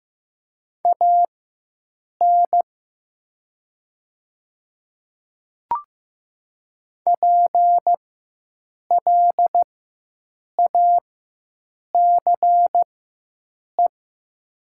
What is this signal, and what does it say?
Morse code, heard as a keyed tone, about 15 wpm, spaced as for 7 wpm, 700 Hz. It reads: AN PLACE
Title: 4x Character Spacing - Top 300 Words - 15wpm